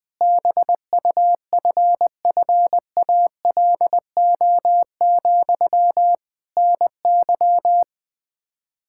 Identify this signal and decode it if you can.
Morse code, heard as a keyed tone, about 20 wpm, 700 Hz. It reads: BUFFALO, NY
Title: Practice QSO - Exchange 1 - Beginning - 20wpm